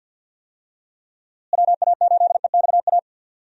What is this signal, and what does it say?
Morse code, heard as a keyed tone, about 47 wpm, 700 Hz. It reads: WA8EXA
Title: USA Call Signs - 6 Characters - Repeated 2x - 50wpm